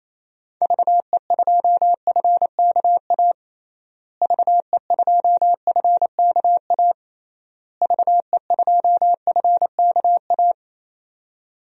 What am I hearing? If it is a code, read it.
Morse code, heard as a keyed tone, about 28 wpm, 700 Hz. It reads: VE2FKA VE2FKA VE2FKA
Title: DXCC-100 Call Signs - 6 Characters - Repeated 3x - 28wpm